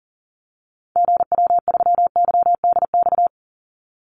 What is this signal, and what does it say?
Morse code, heard as a keyed tone, about 41 wpm, 700 Hz. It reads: GW3YDX
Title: Call Signs - 40wpm